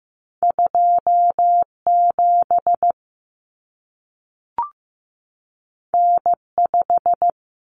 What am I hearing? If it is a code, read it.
Morse code, heard as a keyed tone, about 15 wpm, 700 Hz. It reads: 27 N5
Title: Common Cut Numbers - 2-Digit 15wpm